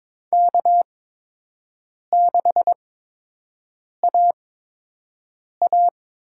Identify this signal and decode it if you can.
Morse code, heard as a keyed tone, about 22 wpm, spaced as for 6 wpm, 700 Hz. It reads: K6AA
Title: USA Call Signs - 5 Characters - 8x Character Spacing - 22wpm